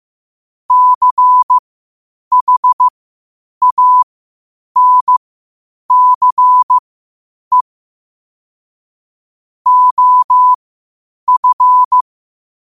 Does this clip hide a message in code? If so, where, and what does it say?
Yes: Morse code from 0.7 to 12.0 s, heard as a keyed tone, about 15 wpm, spaced as for 8 wpm, 1000 Hz. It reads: CHANCE OF